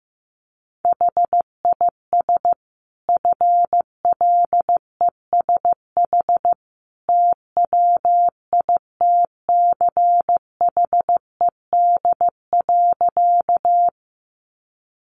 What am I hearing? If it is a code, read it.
Morse code, heard as a keyed tone, about 15 wpm, 700 Hz. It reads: HIS FLESH TWITCHED.
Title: A Sound Of Thunder 15wpm